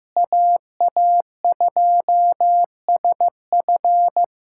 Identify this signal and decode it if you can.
Morse code, heard as a keyed tone, about 16 wpm, 700 Hz. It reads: AA2SF
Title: USA Callsign DE Callsign - 5 Characters - 15wpm